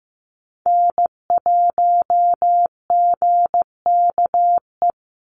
Morse code, heard as a keyed tone, about 15 wpm, 700 Hz. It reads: N1GKE
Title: Call Signs Repeated 3x - 15wpm